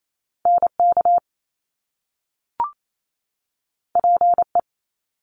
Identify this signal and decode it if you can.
Morse code, heard as a keyed tone, about 28 wpm, 700 Hz. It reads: NK PE
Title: Top Two Letter Combinations in Words 28wpm